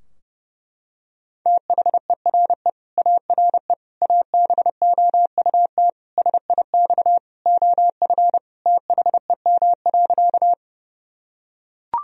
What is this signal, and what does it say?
Morse code, heard as a keyed tone, about 32 wpm, 700 Hz. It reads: THERE ARE ABOUT SIX OF THEM.
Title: Sentences from Top 1000 Words - Review Part 2 - 30wpm